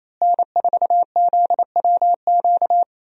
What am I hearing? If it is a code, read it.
Morse code, heard as a keyed tone, about 28 wpm, 700 Hz. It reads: N4ZWQ
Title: USA Call Signs - 5 Characters - Repeated 3x - 28wpm